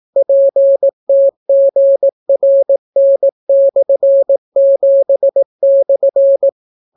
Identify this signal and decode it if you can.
Morse code, heard as a keyed tone, about 18 wpm, 550 Hz. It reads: PTGRN/7/